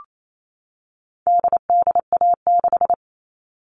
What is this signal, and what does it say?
Morse code, heard as a keyed tone, about 28 wpm, 700 Hz. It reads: DDA6
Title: All Cut Numbers - 4-Digit 28wpm